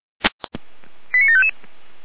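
Five-tone selective call (ZVEI), heard as quick runs of five tones, tones of about 70 ms. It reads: 89755
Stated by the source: ZVEI five-tone selective call received by SDR.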